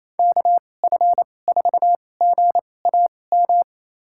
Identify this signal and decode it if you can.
Morse code, heard as a keyed tone, about 28 wpm, spaced as for 20 wpm, 700 Hz. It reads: KF4GAM
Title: USA Call Signs - 2x Character Spacing - 28wpm